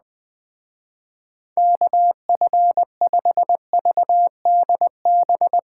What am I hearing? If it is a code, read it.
Morse code, heard as a keyed tone, about 20 wpm, 700 Hz. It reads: KF5VDB